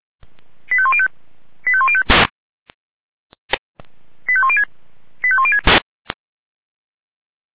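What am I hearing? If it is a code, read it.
Five-tone selective call (ZVEI), heard as quick runs of five tones, tones of about 70 ms. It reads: 85105 85105 85106 85106
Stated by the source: ZVEI five-tone selective call received on an SDR.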